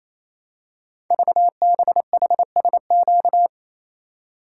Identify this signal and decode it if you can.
Morse code, heard as a keyed tone, about 27 wpm, 700 Hz. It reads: VBHSQ
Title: Code Groups - 5 Letter:Numbers - 28wpm